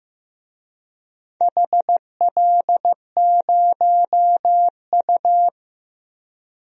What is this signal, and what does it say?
Morse code, heard as a keyed tone, about 15 wpm, 700 Hz. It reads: HL0U
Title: DXCC-100 Call Signs - 4 Characters - 15wpm